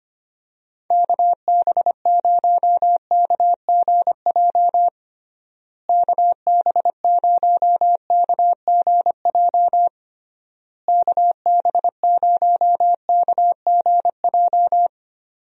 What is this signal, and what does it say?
Morse code, heard as a keyed tone, about 25 wpm, 700 Hz. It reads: KB0KGJ KB0KGJ KB0KGJ
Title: USA Call Signs - 6 Characters - Repeated 3x - 25wpm